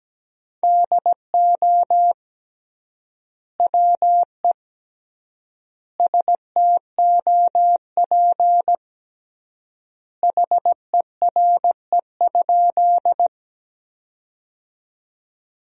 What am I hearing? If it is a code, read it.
Morse code, heard as a keyed tone, about 17 wpm, 700 Hz. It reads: DO WE STOP HERE?
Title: 3x Word-Spacing - Sentences from Top 300 Words - 17wpm